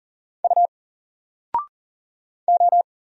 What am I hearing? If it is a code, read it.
Morse code, heard as a keyed tone, about 40 wpm, 700 Hz. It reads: U O
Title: Single Letter-Number 40wpm